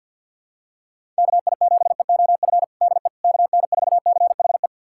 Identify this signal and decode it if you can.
Morse code, heard as a keyed tone, about 50 wpm, 700 Hz. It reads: KI7EOF DE KN4CHE
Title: USA Callsign DE Callsign - 6 Characters - 50wpm